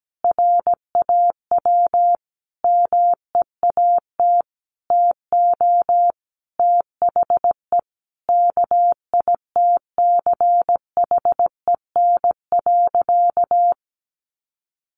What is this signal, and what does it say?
Morse code, heard as a keyed tone, about 17 wpm, 700 Hz. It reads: RAW MEAT TO THE KITCHEN.
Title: Sentences from Top 1500 Words 17wpm